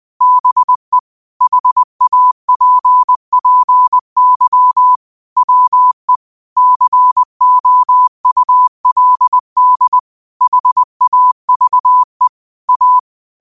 Morse code, heard as a keyed tone, about 20 wpm, 1000 Hz. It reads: BE HAPPY WE COULD HAVE A